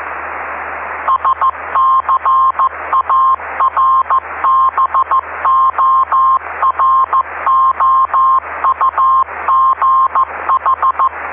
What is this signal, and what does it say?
Morse code, heard as a keyed tone, about 15 wpm, 1060 Hz, with noise. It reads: SCARBOROUGH
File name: ve3we-cw-beacon-28265